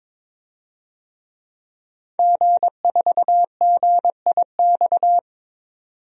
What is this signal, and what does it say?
Morse code, heard as a keyed tone, about 22 wpm, 700 Hz. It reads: G4GIX
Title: DXCC-100 Call Signs - 5 Characters - 22wpm